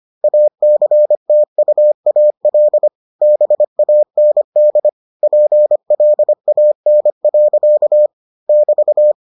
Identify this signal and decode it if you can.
Morse code, heard as a keyed tone, about 25 wpm, 600 Hz. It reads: ACTUAL BAND PLAN. <BT>